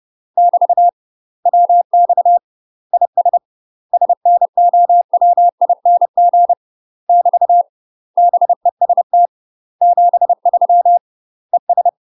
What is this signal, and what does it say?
Morse code, heard as a keyed tone, about 30 wpm, 700 Hz. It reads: <BT> WX IS SNOWING <BT> BEST 73 ES
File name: UK MASTER QSOs TRAINING / 30WPM / UK MASTER QSOs 0027